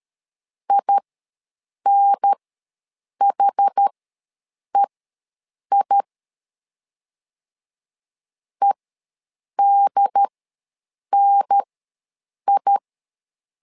Morse code, heard as a keyed tone, about 13 wpm, spaced as for 7 wpm, 800 Hz. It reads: INHEI EDNI
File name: LEZIONI / 2 UHDN / 2 UHDN ws 12wpm 1708